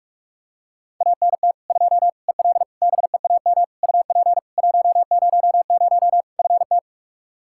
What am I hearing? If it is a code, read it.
Morse code, heard as a keyed tone, about 43 wpm, 700 Hz. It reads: ANT 2 EL BEAM UP 100 FT
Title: Practice QSO - Exchange 2 - Info - 45wpm